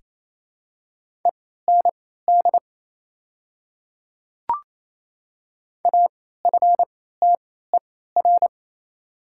Morse code, heard as a keyed tone, about 27 wpm, spaced as for 16 wpm, 700 Hz. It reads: END AFTER
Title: 3x Character Spacing - Top 300 Words - 28wpm